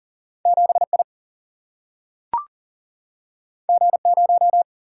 Morse code, heard as a keyed tone, about 40 wpm, 700 Hz. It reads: 7I G0
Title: Single-Letter Single-Number 40wpm